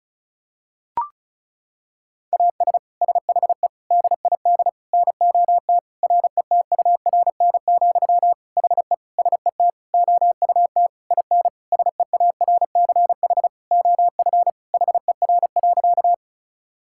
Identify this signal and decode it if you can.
Morse code, heard as a keyed tone, about 34 wpm, 700 Hz. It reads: AS SHE DID NOT RETURN, HE SET OUT IN SEARCH OF HER.
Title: Sentences from Top 1000 Words - Review Part 5 - 35wpm